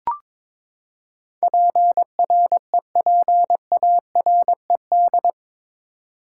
Morse code, heard as a keyed tone, about 22 wpm, 700 Hz. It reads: PREPARED
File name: practice / 22wpm / Eight Letter Words - 22wpm